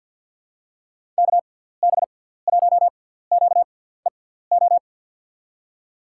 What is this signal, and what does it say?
Morse code, heard as a keyed tone, about 48 wpm, spaced as for 17 wpm, 700 Hz. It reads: KB1QEO
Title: USA Call Signs - 6 Characters - 6x Character Spacing - 50wpm